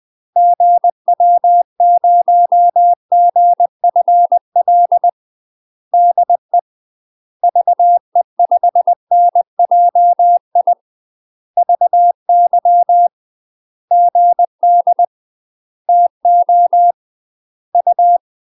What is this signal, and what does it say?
Morse code, heard as a keyed tone, about 20 wpm, 700 Hz. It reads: GW0GFL DE VE5NJI VY GD TO U
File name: UK MASTER QSOs TRAINING / 20WPM / UK MASTER QSOs 0013